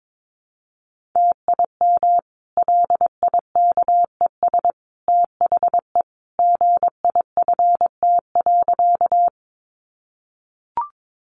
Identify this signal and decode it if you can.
Morse code, heard as a keyed tone, about 22 wpm, 700 Hz. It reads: TIM LIKES THE GIFT.
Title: Easy Phrases 22wpm